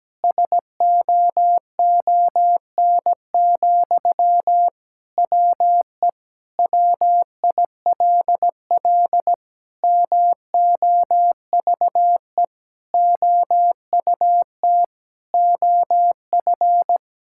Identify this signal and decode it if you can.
Morse code, heard as a keyed tone, about 17 wpm, 700 Hz. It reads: SOON, WE WILL MOVE OUT OF
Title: Sentences from Top 1000 Words - Review Part 6 - 17wpm